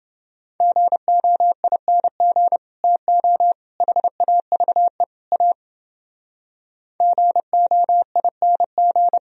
Morse code, heard as a keyed tone, about 30 wpm, 700 Hz. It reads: GOING TO HAVE A GOING